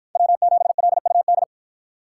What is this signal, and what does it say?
Morse code, heard as a keyed tone, about 53 wpm, 700 Hz. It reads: W7LUD